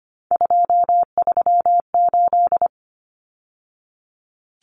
Morse code, heard as a keyed tone, about 25 wpm, 690 Hz. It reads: 238
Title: Numbers - 3-Digit 25wpm